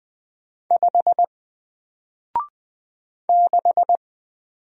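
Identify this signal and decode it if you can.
Morse code, heard as a keyed tone, about 20 wpm, 700 Hz. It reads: 5 6
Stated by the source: Single letter and number Morse drill at 20 wpm, from N4LCD.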